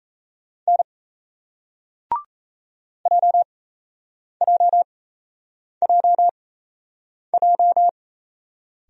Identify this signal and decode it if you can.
Morse code, heard as a keyed tone, about 36 wpm, spaced as for 8 wpm, 700 Hz. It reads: N JJJJ